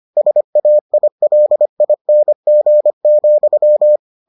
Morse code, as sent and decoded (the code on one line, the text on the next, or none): ... .- .. .-.. .. -. --. --..--
SAILING,